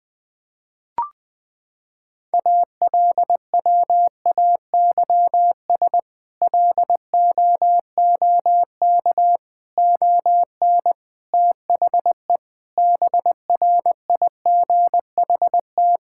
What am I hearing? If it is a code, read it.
Morse code, heard as a keyed tone, about 21 wpm, 700 Hz. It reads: ALWAYS LOOK ON THE BRIGHT
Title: Sentences with 8-Word Phrases - 20wpm